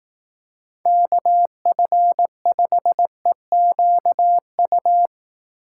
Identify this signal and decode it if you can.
Morse code, heard as a keyed tone, about 18 wpm, 700 Hz. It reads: KF5EQU
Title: Speed Racing - USA 2x3 Call Signs 15wpm